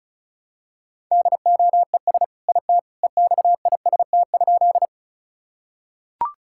Morse code, heard as a keyed tone, about 34 wpm, 700 Hz. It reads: DOES IT EXIST?
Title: Sentences from Top 1000 Words - Review Part 5 - 35wpm